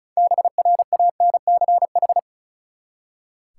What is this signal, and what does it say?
Morse code, heard as a keyed tone, about 34 wpm, 700 Hz. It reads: BRANCH
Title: Six Letter Words - 35wpm